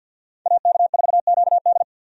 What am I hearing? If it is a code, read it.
Morse code, heard as a keyed tone, about 51 wpm, 700 Hz. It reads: AK4QD
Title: USA Call Signs - 5 Characters - 50wpm